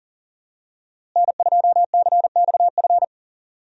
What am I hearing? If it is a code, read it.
Morse code, heard as a keyed tone, about 40 wpm, 700 Hz. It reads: N2CXF
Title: USA Call Signs - 5 Characters - Repeated 3x - 40wpm